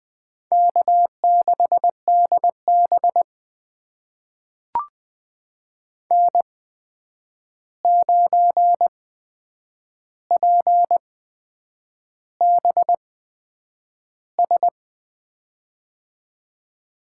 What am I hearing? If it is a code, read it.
Morse code, heard as a keyed tone, about 20 wpm, spaced as for 8 wpm, 700 Hz. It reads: K6DB N9PBS
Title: USA Call Signs - 8x Character Spacing - 20wpm